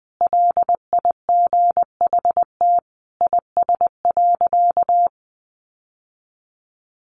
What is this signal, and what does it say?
Morse code, heard as a keyed tone, about 19 wpm, 700 Hz. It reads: LIGHT IS.
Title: Sentences from Top 1000 Words 20wpm